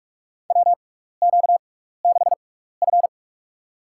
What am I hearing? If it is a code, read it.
Morse code, heard as a keyed tone, about 46 wpm, spaced as for 15 wpm, 700 Hz. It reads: WQ6F
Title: USA Call Signs - 4 Characters - 6x Character Spacing - 45wpm